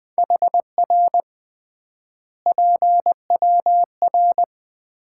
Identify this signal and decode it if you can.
Morse code, heard as a keyed tone, about 20 wpm, 700 Hz. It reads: HR PWR
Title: Common QSO Phrases - 3x Word-Spacing - 20wpm